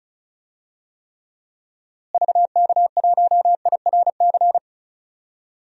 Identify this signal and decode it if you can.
Morse code, heard as a keyed tone, about 34 wpm, 700 Hz. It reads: VK1IRC
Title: DXCC-100 Call Signs - 6 Characters - Repeated 2x - 35wpm